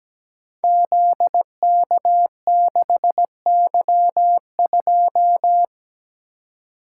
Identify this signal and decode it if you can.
Morse code, heard as a keyed tone, about 17 wpm, 700 Hz. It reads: ZK6Y2
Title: Code Groups - 5 Letter:Numbers - 17wpm